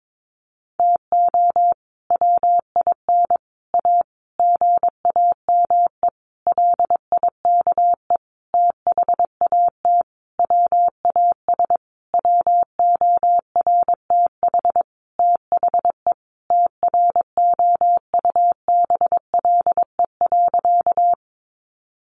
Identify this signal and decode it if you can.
Morse code, heard as a keyed tone, about 22 wpm, 700 Hz. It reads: TO WIN A GAME LIKE THAT WAS WORTH THE TROUBLE.
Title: Sentences from Top 1500 Words 22wpm